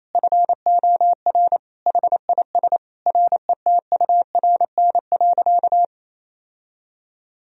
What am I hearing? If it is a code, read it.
Morse code, heard as a keyed tone, about 28 wpm, 700 Hz. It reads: FOR HIS RETURN.
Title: Sentences from Top 1000 Words - Review Part 1 - 28wpm